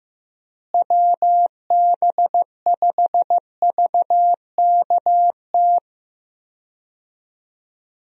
Morse code, heard as a keyed tone, about 15 wpm, 700 Hz. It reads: WB5VKT
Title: USA Call Signs - 6 Characters - Repeated 3x - 15wpm